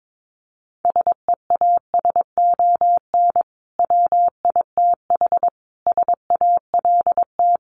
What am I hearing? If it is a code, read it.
Morse code, heard as a keyed tone, about 22 wpm, 700 Hz. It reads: SEASON WITH SALT
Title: Sets of 5 Words 22wpm